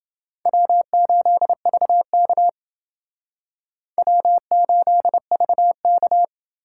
Morse code, heard as a keyed tone, about 28 wpm, 700 Hz. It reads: W8VK W8VK